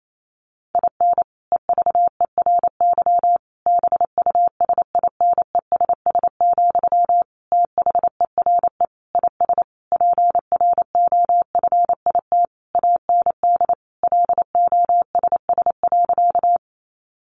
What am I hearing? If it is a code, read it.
Morse code, heard as a keyed tone, about 28 wpm, 700 Hz. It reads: IN EVERY BUSINESS, THERE IS PROFIT AND LOSS.
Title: Sentences from Top 1400 Words 28wpm